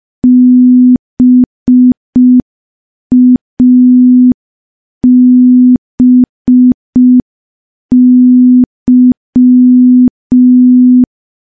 Morse code, heard as a keyed tone, about 5 wpm, 250 Hz. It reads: BABY
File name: morse / morse-training-05WPM